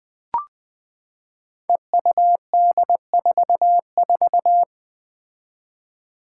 Morse code, heard as a keyed tone, about 20 wpm, 700 Hz. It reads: EUD44
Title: Five Challenging Letter-Number Combinations 20wpm